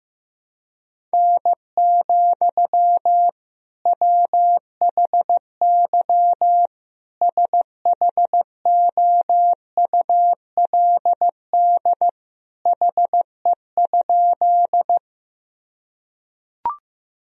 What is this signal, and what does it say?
Morse code, heard as a keyed tone, about 15 wpm, 700 Hz. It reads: N, WHY SHOULD HE?
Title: Sentences from Top 300 Words and Abbreviations 15wpm